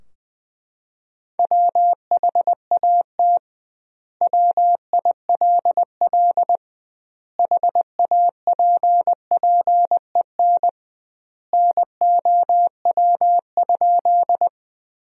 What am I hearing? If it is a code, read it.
Morse code, heard as a keyed tone, about 21 wpm, 700 Hz. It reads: WHAT WILL HAPPEN NOW?
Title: 2x Word-Spacing - Sentences from Top 400 Words - 20wpm